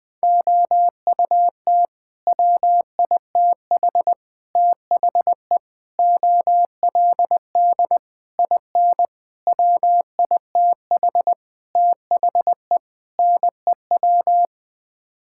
Morse code, heard as a keyed tone, about 20 wpm, 700 Hz. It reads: OUT WITH THE OLD IN WITH THE NEW
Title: Sets of 8 Words - 20wpm